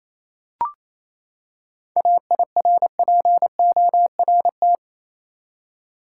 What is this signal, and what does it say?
Morse code, heard as a keyed tone, about 28 wpm, 700 Hz. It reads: AIRPORT
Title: Seven Letter Words - 28wpm